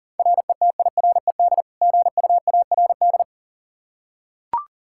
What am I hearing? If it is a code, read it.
Morse code, heard as a keyed tone, about 40 wpm, 700 Hz. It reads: RETIRED GUARD
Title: K1USN SST QSO Practice 40wpm